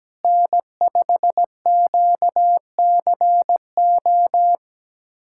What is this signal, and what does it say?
Morse code, heard as a keyed tone, about 17 wpm, 700 Hz. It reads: N5QCO
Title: USA Callsign DE Callsign - 5 Characters - 17wpm